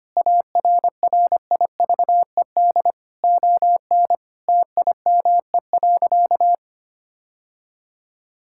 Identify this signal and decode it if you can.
Morse code, heard as a keyed tone, about 25 wpm, 700 Hz. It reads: ARRIVED ON TIME.
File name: practice / 25wpm / Sentences with 2-Word Phrases - 25wpm